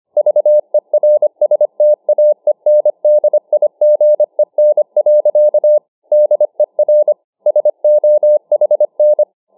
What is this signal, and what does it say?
Morse code, heard as a keyed tone, about 25 wpm, 600 Hz. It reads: VERSTAENDIGEN. DER SOHN
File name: Morsehandicap 25WpM